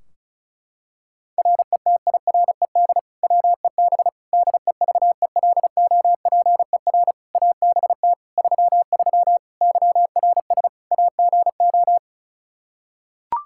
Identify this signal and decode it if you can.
Morse code, heard as a keyed tone, about 36 wpm, 700 Hz. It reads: RETIRED WEB DEVELOPER ABT 33 YRS AGO
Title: K1USN SST QSO Practice 35wpm